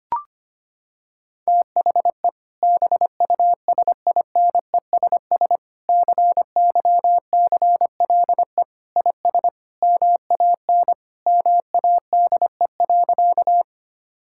Morse code, heard as a keyed tone, about 25 wpm, 700 Hz. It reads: THE BUSINESS CYCLE IS MAN MADE.
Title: Sentences from Top 1000 Words - Review Part 6 - 25wpm